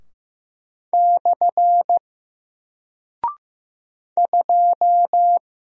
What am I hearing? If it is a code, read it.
Morse code, heard as a keyed tone, about 15 wpm, 700 Hz. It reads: / 2
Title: Single Character (Letter:Number:Punctuation:Prosign) Alt 15wpm